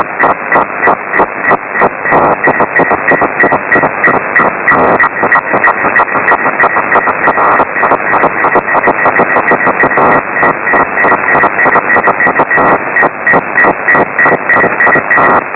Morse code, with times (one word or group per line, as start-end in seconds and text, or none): none